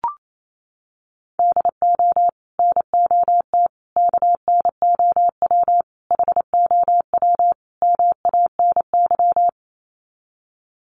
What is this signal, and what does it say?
Morse code, heard as a keyed tone, about 28 wpm, 700 Hz. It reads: DO NOT KNOW HOW MANY
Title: Sets of 5 Words Encore 28wpm